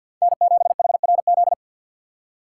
Morse code, heard as a keyed tone, about 53 wpm, 690 Hz. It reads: N7HRZ
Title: USA Call Signs - 4x Character Spacing - 50wpm